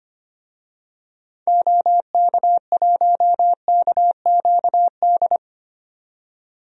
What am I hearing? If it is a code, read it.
Morse code, heard as a keyed tone, about 25 wpm, 690 Hz. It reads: OK1KQD
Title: DXCC-100 Call Signs - 6 Characters - 25wpm